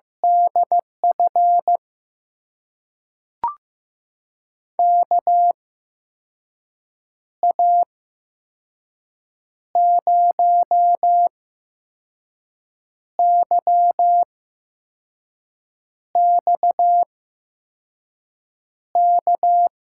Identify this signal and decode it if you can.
Morse code, heard as a keyed tone, about 15 wpm, spaced as for 5 wpm, 700 Hz. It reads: DF KA0YXK
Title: USA Call Signs - 6 Characters - 8x Character Spacing - 15wpm